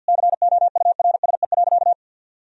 Morse code, heard as a keyed tone, about 52 wpm, 700 Hz. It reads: COURSE.
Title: Sentences from Top 1000 Words - Review Part 1 - 50wpm